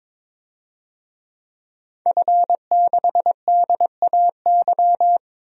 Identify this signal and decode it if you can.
Morse code, heard as a keyed tone, about 22 wpm, 700 Hz. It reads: F6DAY
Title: DXCC-100 Call Signs - 5 Characters - Repeated 3x - 22wpm